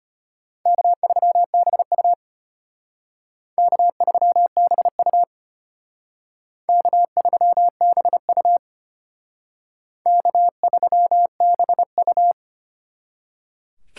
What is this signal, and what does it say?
Morse code, heard as a keyed tone, about 33 wpm, 700 Hz. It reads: K3BU K3BU K3BU K3BU